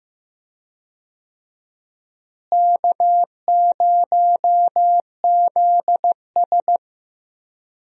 Morse code, heard as a keyed tone, about 15 wpm, 700 Hz. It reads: K0ZS
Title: USA Call Signs - 4 Characters - Repeated 2x - 15wpm